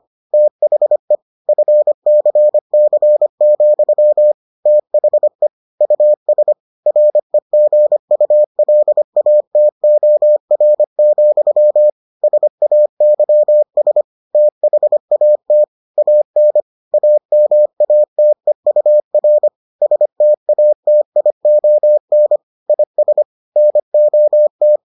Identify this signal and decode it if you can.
Morse code, heard as a keyed tone, about 25 wpm, 600 Hz. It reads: THE FCC, THE US REGULATOR, SAYS THAT AN AMATEUR STATION IS NOT